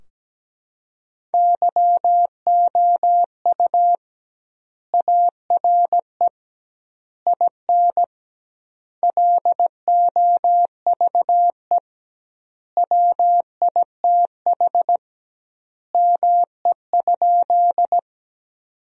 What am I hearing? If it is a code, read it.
Morse code, heard as a keyed tone, about 17 wpm, 700 Hz. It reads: YOU ARE IN LOVE WITH ME?
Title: 2x Word-Spacing - Sentences from Top 400 Words - 17wpm